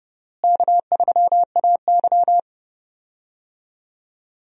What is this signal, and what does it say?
Morse code, heard as a keyed tone, about 30 wpm, 700 Hz. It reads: K3AY
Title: USA Call Signs - 4 Characters - Repeated 2x - 30wpm